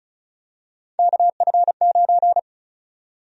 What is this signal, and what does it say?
Morse code, heard as a keyed tone, about 36 wpm, 700 Hz. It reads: KF9
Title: USA Call Sign Prefixes - Repeated 2x - 35wpm